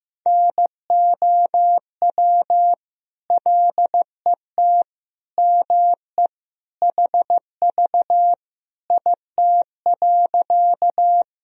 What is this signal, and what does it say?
Morse code, heard as a keyed tone, about 15 wpm, 700 Hz. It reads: NOW LET ME HV IT.
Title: Sentences from Top 300 Words and Abbreviations 15wpm